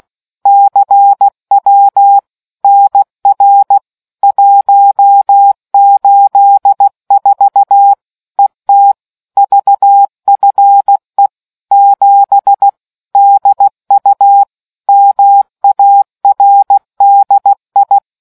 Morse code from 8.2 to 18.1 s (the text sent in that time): ET VFE 7 DU MARDI